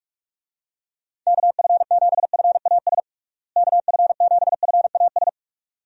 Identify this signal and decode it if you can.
Morse code, heard as a keyed tone, about 42 wpm, 700 Hz. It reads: KF7FAS KF7FAS
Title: USA Call Signs - 6 Characters - Repeated 2x - 45wpm